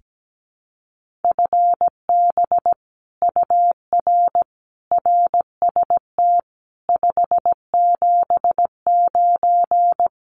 Morse code, heard as a keyed tone, about 17 wpm, 700 Hz. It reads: FB UR RST 579
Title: Common QSO Phrases Encore 17wpm